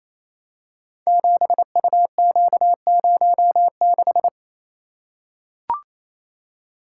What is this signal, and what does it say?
Morse code, heard as a keyed tone, about 28 wpm, 700 Hz. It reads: 7UQ06
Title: Code Groups - 5 Letter:Numbers - 28wpm